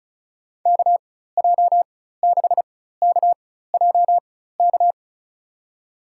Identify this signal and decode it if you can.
Morse code, heard as a keyed tone, about 34 wpm, spaced as for 16 wpm, 700 Hz. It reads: KJ6KJK
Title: USA Call Signs - 6 Characters - 4x Character Spacing - 35wpm